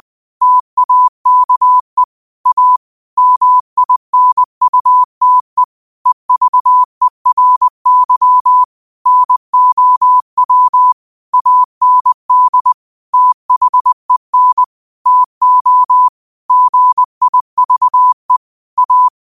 Morse code, heard as a keyed tone, about 20 wpm, 1000 Hz. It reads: TAKE A MINUTE EVERY NOW AND THEN TO GIVE A
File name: qotdfiles / 20wpm